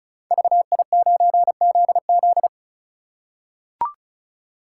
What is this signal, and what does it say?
Morse code, heard as a keyed tone, about 36 wpm, 700 Hz. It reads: VI9ZZ
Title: Code Groups - 5 Letter:Numbers - 35wpm